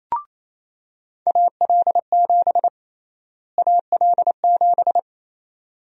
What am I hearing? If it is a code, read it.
Morse code, heard as a keyed tone, about 28 wpm, 700 Hz. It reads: AL7 AL7
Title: USA Call Sign Prefixes - Repeated 2x - 28wpm